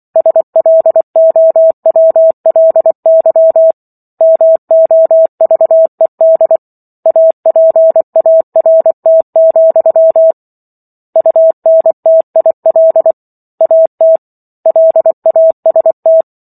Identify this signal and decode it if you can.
Morse code, heard as a keyed tone, about 24 wpm, 650 Hz. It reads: SLOWLY MOVED APART, UNTIL AT LAST